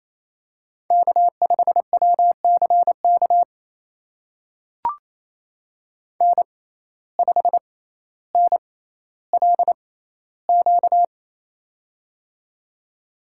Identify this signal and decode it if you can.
Morse code, heard as a keyed tone, about 27 wpm, spaced as for 14 wpm, 700 Hz. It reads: K5WCK N5NLQ
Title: USA Call Signs - 5 Characters - 6x Character Spacing - 28wpm